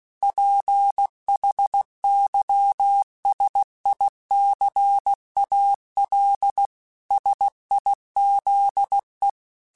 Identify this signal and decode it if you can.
Morse code, heard as a keyed tone, about 16 wpm, 800 Hz. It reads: PHYSICAL SIZE
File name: Technical Subjects / Technical Subjects-mp3s / Sw Pwr Sup @15